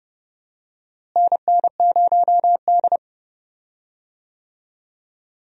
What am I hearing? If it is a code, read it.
Morse code, heard as a keyed tone, about 30 wpm, 700 Hz. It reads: NN0D
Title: USA Call Signs - 4 Characters - Repeated 2x - 30wpm